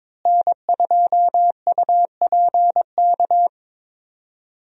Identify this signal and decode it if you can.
Morse code, heard as a keyed tone, about 23 wpm, 700 Hz. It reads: N2UPK
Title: USA Call Signs - 5 Characters - Repeated 3x - 22wpm